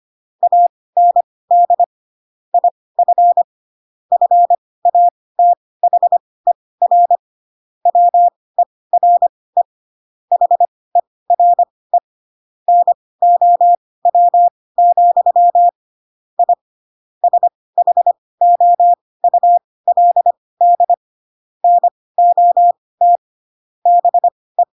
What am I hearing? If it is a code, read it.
Morse code, heard as a keyed tone, about 25 wpm, spaced as for 18 wpm, 700 Hz. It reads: AND IF FATHER WERE HERE NOW, I SHOULD NOT BE